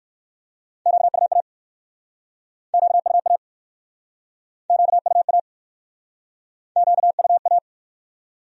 Morse code, heard as a keyed tone, about 56 wpm, 700 Hz. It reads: QUA QUA QUA QUA